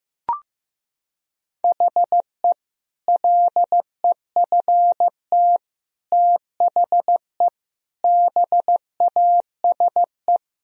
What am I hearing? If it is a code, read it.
Morse code, heard as a keyed tone, about 15 wpm, 700 Hz. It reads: HE LEFT THE BASE
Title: Sentences from Top 1000 Words - Review Part 1 - 15wpm